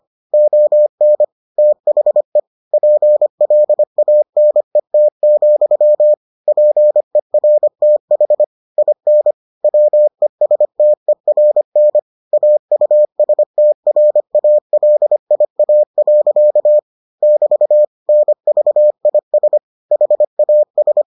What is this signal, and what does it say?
Morse code, heard as a keyed tone, about 25 wpm, 600 Hz. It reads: ON THE PLANET, PERTH IN WESTERN AUSTRALIA. <BT> NVIS HAS